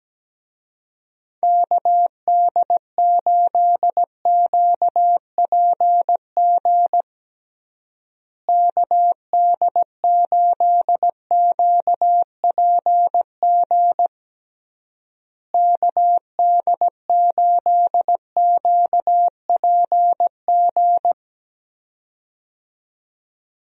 Morse code, heard as a keyed tone, about 17 wpm, 700 Hz. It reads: KD8QPG KD8QPG KD8QPG
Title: USA Call Signs - 6 Characters - Repeated 3x - 17wpm